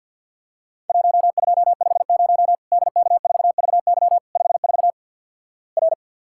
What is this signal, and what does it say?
Morse code, heard as a keyed tone, about 50 wpm, 700 Hz. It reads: 1250 DK4VY 54 R
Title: Worked All Europe DX Contest - QTC Exchange - 50wpm